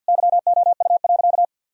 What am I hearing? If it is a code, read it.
Morse code, heard as a keyed tone, about 51 wpm, 700 Hz. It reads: YOU.
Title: Sentences from Top 1000 Words - Review Part 4 - 50wpm